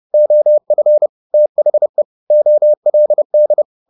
Morse code, heard as a keyed tone, about 30 wpm, 600 Hz. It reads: OF THE OLD
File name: the old man and the sea / the old man and the sea 1 30wpm